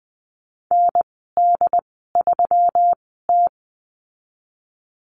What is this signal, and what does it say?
Morse code, heard as a keyed tone, about 20 wpm, spaced as for 14 wpm, 700 Hz. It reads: ND3T
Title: 122 Lesson 20g - Number 3 - Call Signs 20wpm